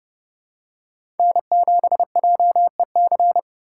Morse code, heard as a keyed tone, about 30 wpm, 700 Hz. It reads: N7JEC
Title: USA Call Signs - 5 Characters - 6x Character Spacing - 30wpm